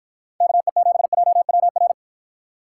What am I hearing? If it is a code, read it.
Morse code, heard as a keyed tone, about 53 wpm, 700 Hz. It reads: KE7JWR